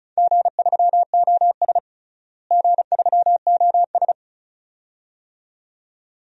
Morse code, heard as a keyed tone, about 36 wpm, 690 Hz. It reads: G3OS G3OS
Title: DXCC-100 Call Signs - 4 Characters - Repeated 2x - 35wpm